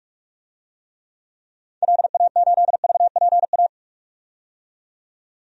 Morse code, heard as a keyed tone, about 45 wpm, 700 Hz. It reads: LA8VPA
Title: DXCC-100 Call Signs - 6 Characters - 45wpm